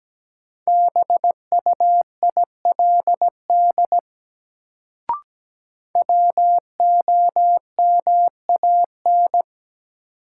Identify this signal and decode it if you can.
Morse code, heard as a keyed tone, about 17 wpm, 700 Hz. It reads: BUILD WOMAN
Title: Sentences from Top 1100 Words - New Word Drill Alt - 17wpm